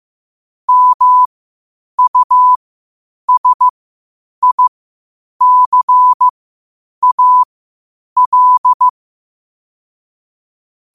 Morse code, heard as a keyed tone, about 15 wpm, spaced as for 8 wpm, 1000 Hz. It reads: MUSICAL
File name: qotdfiles / 8wpm